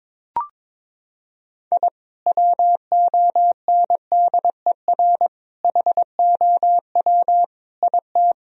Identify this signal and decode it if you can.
Morse code, heard as a keyed tone, about 22 wpm, 700 Hz. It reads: I WONDER HOW IT
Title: Sentences from Top 1000 Words - Review Part 3 - 22wpm